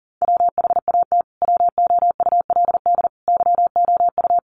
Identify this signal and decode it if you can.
Morse code, heard as a keyed tone, about 40 wpm, 700 Hz. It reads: WHAT WOULD YOU
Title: Sentences from Top 1200 Words 40wpm